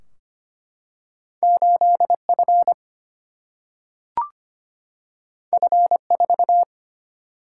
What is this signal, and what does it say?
Morse code, heard as a keyed tone, about 25 wpm, 700 Hz. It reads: 8 F F 4
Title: Single-Letter Single-Number 25wpm